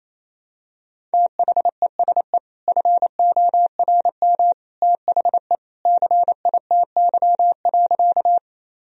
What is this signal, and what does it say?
Morse code, heard as a keyed tone, about 28 wpm, 700 Hz. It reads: THESE FORM THE CITY.
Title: Sentences from Top 1000 Words - Review Part 1 - 28wpm